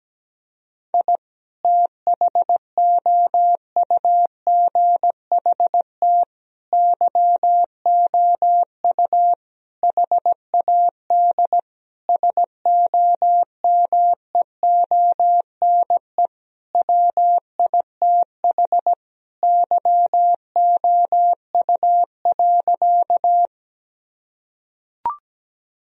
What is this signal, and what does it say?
Morse code, heard as a keyed tone, about 17 wpm, 700 Hz. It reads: I THOUGHT YOU HAD SOMEONE WITH YOU.
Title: Sentences from Top 1000 Words - Review Part 4 - 17wpm